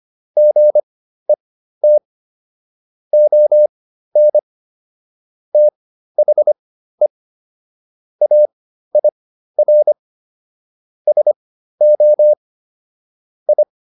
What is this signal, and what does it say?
Morse code, heard as a keyed tone, about 25 wpm, spaced as for 13 wpm, 600 Hz. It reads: GET ON THE AIR SO I